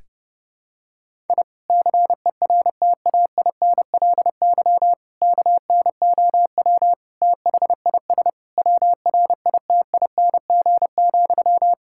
Morse code, heard as a keyed tone, about 30 wpm, 700 Hz. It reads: I CERTAINLY KNOW THIS WRITING,
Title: Sentences from Top 1000 Words - Review Part 6 - 30wpm